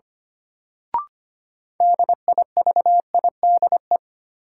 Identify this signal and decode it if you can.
Morse code, heard as a keyed tone, about 25 wpm, 700 Hz. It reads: DIVIDE
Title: Sentences from Top 1600 Words - New Word Drill - 25wpm